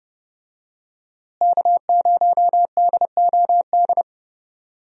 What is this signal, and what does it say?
Morse code, heard as a keyed tone, about 30 wpm, 700 Hz. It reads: K0DOD
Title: USA Call Signs - 5 Characters - Repeated 3x - 30wpm